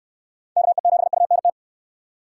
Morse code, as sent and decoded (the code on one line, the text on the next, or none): -.- . --... ..- -. -
KE7UNT